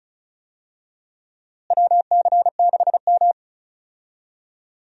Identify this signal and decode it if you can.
Morse code, heard as a keyed tone, about 36 wpm, 700 Hz. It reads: WC6M
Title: USA Call Signs - 4 Characters - Repeated 2x - 35wpm